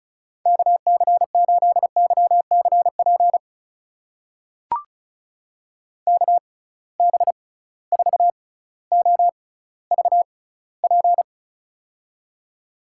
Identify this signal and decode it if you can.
Morse code, heard as a keyed tone, about 36 wpm, spaced as for 18 wpm, 700 Hz. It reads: KC8YCP KB4OVP
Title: USA Call Signs - 6 Characters - 6x Character Spacing - 35wpm